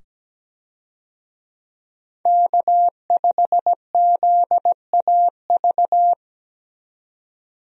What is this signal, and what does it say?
Morse code, heard as a keyed tone, about 17 wpm, 700 Hz. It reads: K5ZAV
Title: USA Call Signs - 5 Characters - 17wpm